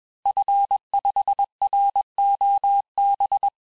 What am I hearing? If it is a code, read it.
Morse code, heard as a keyed tone, about 21 wpm, 800 Hz. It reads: F5ROB
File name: teste-FIN-de-QSO-vitesse-20mpm